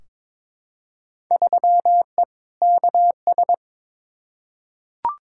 Morse code, heard as a keyed tone, about 22 wpm, 700 Hz. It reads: 3E KS
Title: Field Day Exchanges Encore - 22wpm